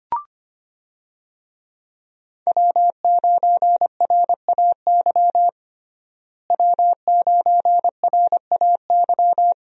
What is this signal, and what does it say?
Morse code, heard as a keyed tone, about 25 wpm, 700 Hz. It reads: W9RAY W9RAY
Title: USA Call Signs - 5 Characters - Repeated 2x - 25wpm